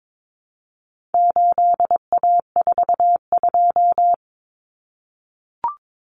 Morse code, heard as a keyed tone, about 22 wpm, 700 Hz. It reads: 8A42
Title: Common Cut Numbers - 4-Digit 22wpm